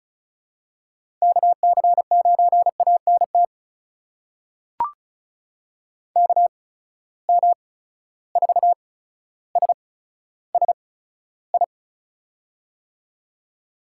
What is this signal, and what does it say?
Morse code, heard as a keyed tone, about 36 wpm, spaced as for 15 wpm, 700 Hz. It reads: KC9ANT KM4SSI
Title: USA Call Signs - 6 Characters - 8x Character Spacing - 35wpm